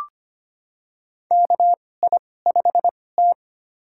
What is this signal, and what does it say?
Morse code, heard as a keyed tone, about 25 wpm, spaced as for 18 wpm, 700 Hz. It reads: KI5T
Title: USA Call Signs - 2x Character Spacing - 25wpm